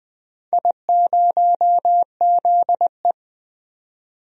Morse code, heard as a keyed tone, about 21 wpm, 700 Hz. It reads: I0ZE
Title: DXCC-100 Call Signs - 4 Characters - 20wpm